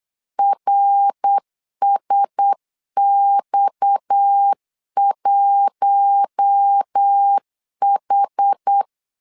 Morse code, heard as a keyed tone, about 8 wpm, 800 Hz. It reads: RSX1H
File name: LEZIONI / 6 PKX0 / 6 PKX 8wpm 3041